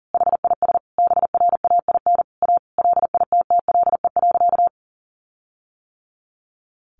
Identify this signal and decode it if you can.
Morse code, heard as a keyed tone, about 40 wpm, 700 Hz. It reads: HIS BRAIN A LITTLE.
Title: Sentences from Top 1500 Words 40wpm